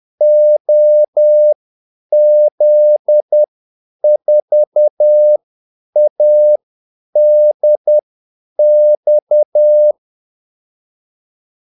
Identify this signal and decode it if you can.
Morse code, heard as a keyed tone, about 10 wpm, spaced as for 8 wpm, 600 Hz. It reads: OZ4ADX
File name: Callsigns-10-8wpm